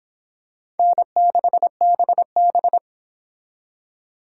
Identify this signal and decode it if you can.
Morse code, heard as a keyed tone, about 26 wpm, 700 Hz. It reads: N6BB